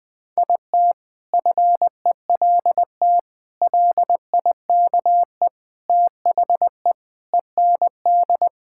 Morse code, heard as a keyed tone, about 20 wpm, 700 Hz. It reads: IT FELT LIKE THE END
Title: Sentences with 8-Word Phrases - Spoken Beginning - 20wpm